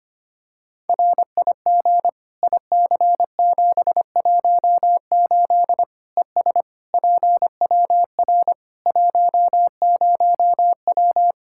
Morse code, heard as a keyed tone, about 25 wpm, 700 Hz. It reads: RIG IC718 ES PWR 10W
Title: Practice QSO - Exchange 2 - Info - 25wpm